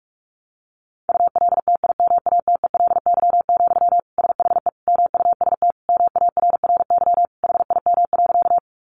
Morse code, heard as a keyed tone, about 44 wpm, 700 Hz. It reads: ULTIMATELY, SHE MUST MARRY HIM.
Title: Sentences from Top 1200 Words 45wpm